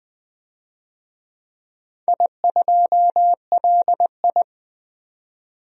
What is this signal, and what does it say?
Morse code, heard as a keyed tone, about 20 wpm, 700 Hz. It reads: I2LI
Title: DXCC-100 Call Signs - 4 Characters - 20wpm